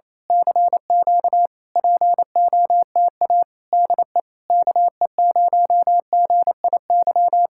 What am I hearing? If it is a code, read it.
Morse code, heard as a keyed tone, about 27 wpm, 700 Hz. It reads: CQ POTA DE KE0GIY
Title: Speed-Racing Common QSO Phrases Encore - 28wpm